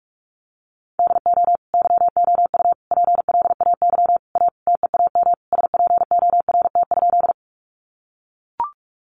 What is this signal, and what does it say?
Morse code, heard as a keyed tone, about 45 wpm, 700 Hz. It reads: DO YOU PLAY A TEAM SPORT?
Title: Sentences from Top 1500 Words 45wpm